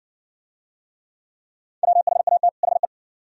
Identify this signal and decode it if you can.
Morse code, heard as a keyed tone, about 50 wpm, 700 Hz. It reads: WHAT HE